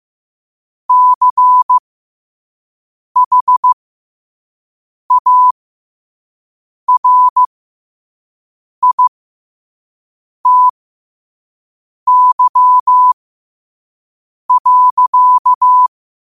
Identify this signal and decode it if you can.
Morse code, heard as a keyed tone, about 15 wpm, spaced as for 5 wpm, 1000 Hz. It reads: CHARITY.
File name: qotdfiles / 5wpm